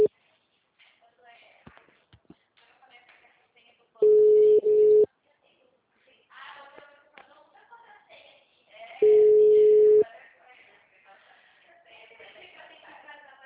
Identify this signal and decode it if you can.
Ringback tone (Europe, 425 Hz), with speech.
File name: out- -2001- - - .3527